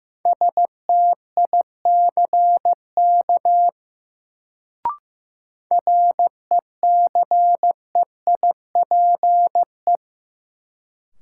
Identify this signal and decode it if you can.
Morse code, heard as a keyed tone, about 15 wpm, 700 Hz. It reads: STICK RECEIPE
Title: Sentences from Top 1200 Words - New Word Drill Alt - 15wpm